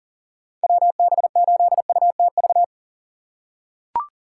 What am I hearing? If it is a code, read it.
Morse code, heard as a keyed tone, about 40 wpm, 700 Hz. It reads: WB8UTV
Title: USA Call Signs - 4x Character Spacing - 40wpm